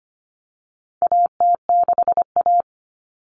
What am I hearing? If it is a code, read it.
Morse code, heard as a keyed tone, about 25 wpm, 700 Hz. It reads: AT6A
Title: Common Cut Numbers - 4-Digit 25wpm